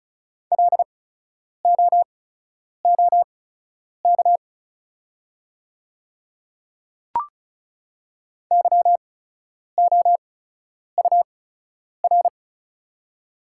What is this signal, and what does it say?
Morse code, heard as a keyed tone, about 36 wpm, spaced as for 10 wpm, 700 Hz. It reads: LOOK YOUR
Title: 8x Character Spacing - Top 100 Words - 35wpm